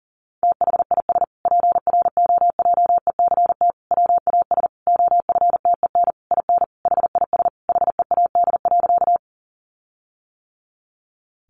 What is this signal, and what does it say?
Morse code, heard as a keyed tone, about 40 wpm, 700 Hz. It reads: THIS PROJECT WAS OFTEN IN HIS HEAD.
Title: Sentences from Top 1400 Words 40wpm